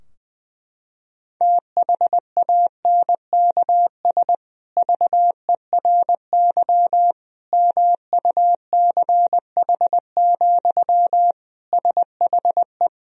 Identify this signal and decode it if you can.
Morse code, heard as a keyed tone, about 20 wpm, 700 Hz. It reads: THANKS VERY MUCH, SHE
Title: Sentences from Top 1000 Words - Review Part 3 - 20wpm